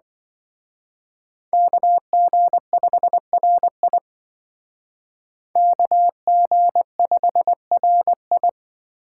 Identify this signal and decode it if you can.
Morse code, heard as a keyed tone, about 24 wpm, 700 Hz. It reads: KG5RI KG5RI
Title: Speed Racing - USA 2x2 Call Signs 20wpm